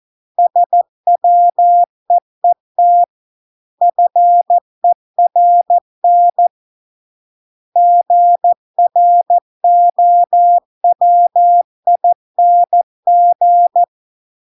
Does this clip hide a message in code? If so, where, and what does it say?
Yes: Morse code from 0.4 to 13.8 s, heard as a keyed tone, about 14 wpm, 700 Hz. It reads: SWEET FERN GROWING